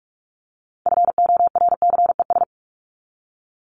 Morse code, heard as a keyed tone, about 44 wpm, 700 Hz. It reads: FORCES
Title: ES:S Suffix - Top 100 Words with Base Words 45wpm